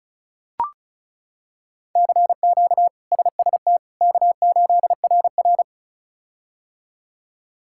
Morse code, heard as a keyed tone, about 34 wpm, 700 Hz. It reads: CQ SST K8RR
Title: K1USN SST QSO Practice - 35wpm